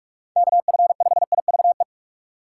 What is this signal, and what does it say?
Morse code, heard as a keyed tone, about 45 wpm, 700 Hz. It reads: KF5IVE
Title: USA Call Signs - 6 Characters - 6x Character Spacing - 45wpm